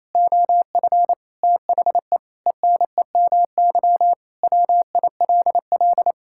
Morse code, heard as a keyed tone, about 27 wpm, 700 Hz. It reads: OF THE ENEMY WILL
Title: Pangram Challenge v2 - 28wpm